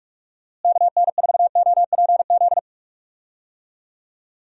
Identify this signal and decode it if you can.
Morse code, heard as a keyed tone, about 46 wpm, 700 Hz. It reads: KN4OPZ
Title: USA Call Signs - 6 Characters - Repeated 2x - 45wpm